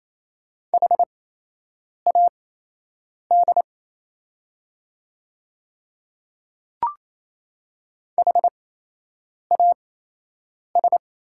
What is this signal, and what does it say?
Morse code, heard as a keyed tone, about 27 wpm, spaced as for 8 wpm, 700 Hz. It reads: HAD HAS
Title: 8x Character Spacing - Top 100 Words - 28wpm